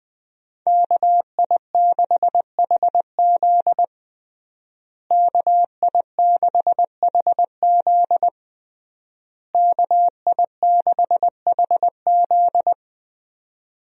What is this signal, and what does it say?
Morse code, heard as a keyed tone, about 20 wpm, 700 Hz. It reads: KI6HZ KI6HZ KI6HZ
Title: USA Call Signs - 5 Characters - Repeated 3x - 20wpm